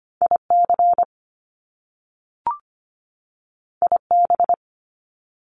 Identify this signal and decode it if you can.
Morse code, heard as a keyed tone, about 25 wpm, 700 Hz. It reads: IC IB
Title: Top Two Letter Combinations in Words 25wpm